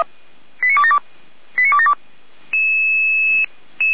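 Five-tone selective call (ZVEI), heard as quick runs of five tones, tones of about 80 ms. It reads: 89272 89272
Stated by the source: ZVEI five-tone selective call 89270 received on SDR.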